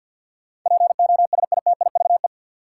Morse code, heard as a keyed tone, about 53 wpm, 700 Hz. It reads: POSITIVE
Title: Sentences from Top 1000 Words - Review Part 5 - New Word Drill - 50wpm